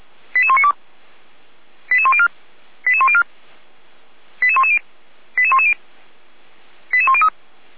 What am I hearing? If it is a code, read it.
Five-tone selective call (ZVEI), heard as quick runs of five tones, tones of about 70 ms. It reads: 88192 88104 88104 88110 88110 88103